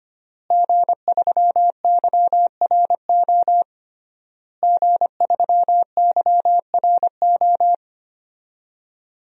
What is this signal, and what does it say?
Morse code, heard as a keyed tone, about 25 wpm, 700 Hz. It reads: G3YRO G3YRO
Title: DXCC-100 Call Signs - 5 Characters - Repeated 3x - 25wpm